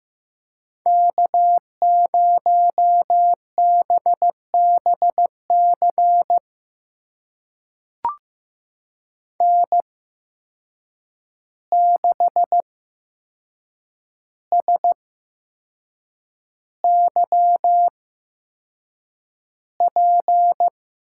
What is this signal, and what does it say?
Morse code, heard as a keyed tone, about 15 wpm, spaced as for 6 wpm, 700 Hz. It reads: K0BBC N6SYP
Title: USA Call Signs - 5 Characters - 8x Character Spacing - 15wpm